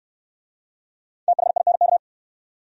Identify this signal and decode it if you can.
Morse code, heard as a keyed tone, about 60 wpm, 700 Hz. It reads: THINK